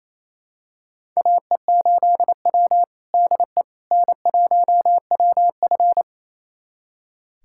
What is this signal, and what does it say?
Morse code, heard as a keyed tone, about 28 wpm, 700 Hz. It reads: AE8W DE N1WF
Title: USA Callsign DE Callsign - 4 Characters - 28wpm